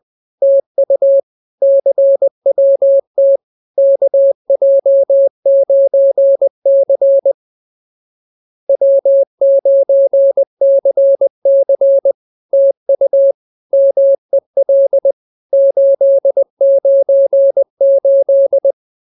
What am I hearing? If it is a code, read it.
Morse code, heard as a keyed tone, about 20 wpm, 550 Hz. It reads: TU CWT KJ9C W9CC TU MEL 898